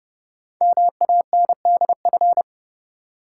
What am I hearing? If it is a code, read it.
Morse code, heard as a keyed tone, about 30 wpm, 700 Hz. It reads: MANDF
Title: Code Groups - 5 Letters - 30wpm